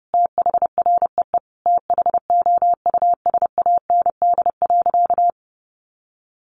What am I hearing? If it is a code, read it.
Morse code, heard as a keyed tone, about 30 wpm, 700 Hz. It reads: THREE THOUSAND.
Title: Sentences from Top 1300 Words 30wpm